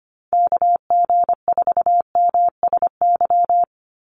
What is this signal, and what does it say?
Morse code, heard as a keyed tone, about 24 wpm, 700 Hz. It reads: KG4MSY
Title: USA Call Signs 25wpm